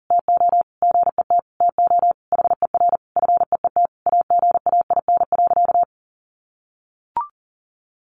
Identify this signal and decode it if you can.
Morse code, heard as a keyed tone, about 40 wpm, 700 Hz. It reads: TO GET TO HER FEET AGAIN.
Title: Sentences from Top 1300 Words 40wpm